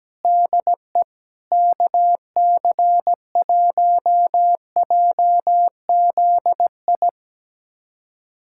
Morse code, heard as a keyed tone, about 17 wpm, 700 Hz. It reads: DE KC1JZI
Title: USA Callsign DE Callsign - 6 Characters - 17wpm